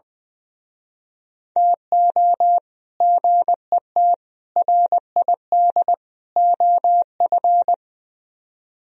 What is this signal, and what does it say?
Morse code, heard as a keyed tone, about 22 wpm, 700 Hz. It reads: TO GET RID OF
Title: Speed-Racing - Sets of 4 Words 20wpm